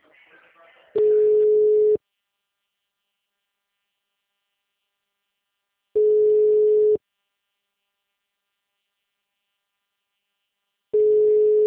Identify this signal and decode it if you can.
Ringback tone (Europe, 425 Hz), with speech.